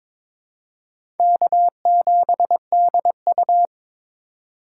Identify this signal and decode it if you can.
Morse code, heard as a keyed tone, about 22 wpm, 700 Hz. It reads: K7DU
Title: USA Call Signs - 4 Characters - 4x Character Spacing - 22wpm